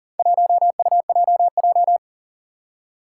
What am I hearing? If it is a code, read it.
Morse code, heard as a keyed tone, about 40 wpm, 700 Hz. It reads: 1UJJ
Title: Four Challenging Letter-Number Combinations 40wpm